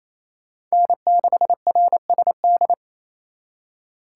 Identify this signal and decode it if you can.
Morse code, heard as a keyed tone, about 36 wpm, 700 Hz. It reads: N6RSD